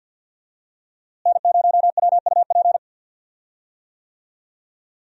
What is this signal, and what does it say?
Morse code, heard as a keyed tone, about 53 wpm, 700 Hz. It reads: N0WUP
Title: USA Call Signs - 5 Characters - Repeated 2x - 50wpm